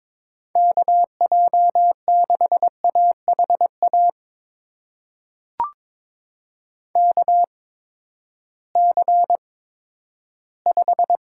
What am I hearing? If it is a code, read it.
Morse code, heard as a keyed tone, about 22 wpm, spaced as for 13 wpm, 700 Hz. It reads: KJ6AHA KC5
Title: USA Call Signs - 6 Characters - 8x Character Spacing - 22wpm